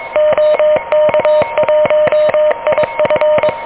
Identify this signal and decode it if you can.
Morse code, heard as a keyed tone, about 22 wpm, 600 Hz, with noise. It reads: OK1IF
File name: OK QRP 2023 / OK QRP 2023